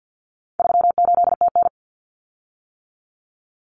Morse code, heard as a keyed tone, about 50 wpm, 700 Hz. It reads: 38TN
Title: Common Cut Numbers - 4-Digit 50wpm